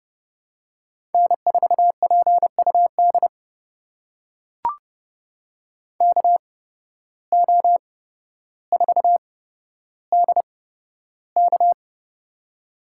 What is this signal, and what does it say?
Morse code, heard as a keyed tone, about 30 wpm, spaced as for 13 wpm, 700 Hz. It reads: N4PUD KO4DK
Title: USA Call Signs - 5 Characters - 8x Character Spacing - 30wpm